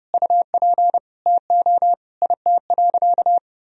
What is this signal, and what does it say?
Morse code, heard as a keyed tone, about 32 wpm, 700 Hz. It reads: UP TO IT.
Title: Sentences from Top 1000 Words - Review Part 2 - 30wpm